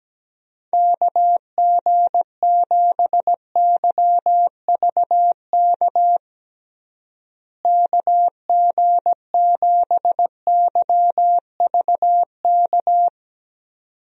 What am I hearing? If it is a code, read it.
Morse code, heard as a keyed tone, about 17 wpm, 700 Hz. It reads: KG7YVK KG7YVK
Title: USA Call Signs - 6 Characters - Repeated 2x - 17wpm